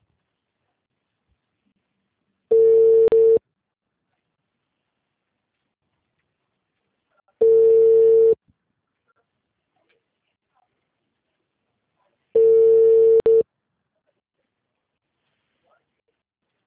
Ringback tone (China, 450 Hz).
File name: out- -2005- - - .33324